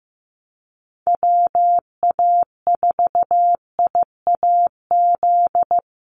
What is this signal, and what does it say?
Morse code, heard as a keyed tone, about 15 wpm, 700 Hz. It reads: WA4IAZ
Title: USA Call Signs Repeated 3x 15wpm